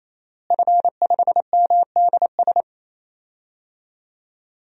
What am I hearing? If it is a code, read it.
Morse code, heard as a keyed tone, about 28 wpm, 700 Hz. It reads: F5MDS
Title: DXCC-100 Call Signs - 5 Characters - Repeated 3x - 28wpm